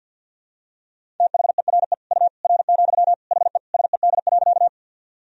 Morse code, heard as a keyed tone, about 50 wpm, 700 Hz. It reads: THERE U R, HE SED.
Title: Sentences from Top 100 Words and Abbreviations 50wpm